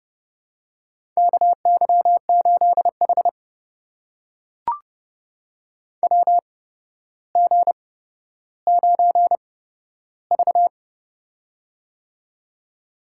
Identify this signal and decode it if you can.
Morse code, heard as a keyed tone, about 30 wpm, spaced as for 13 wpm, 700 Hz. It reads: KY8H WG9V
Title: USA Call Signs - 4 Characters - 8x Character Spacing - 30wpm